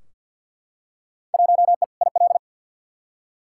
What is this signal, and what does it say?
Morse code, heard as a keyed tone, about 51 wpm, 700 Hz. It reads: 1E IL
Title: Field Day Exchanges - 50wpm